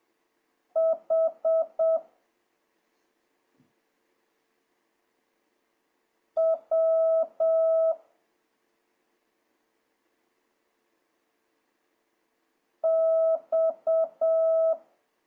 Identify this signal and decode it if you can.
Morse code, heard as a keyed tone, about 7 wpm, 1300 Hz. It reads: H W X
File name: jh8cbh / 2026cw